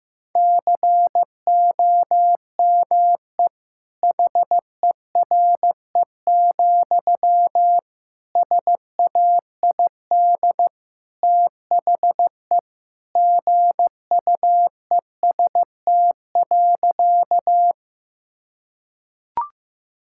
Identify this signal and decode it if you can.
Morse code, heard as a keyed tone, about 16 wpm, 700 Hz. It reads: COME HERE, SAID THE GUEST.
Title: Sentences from Top 1600 Words 15wpm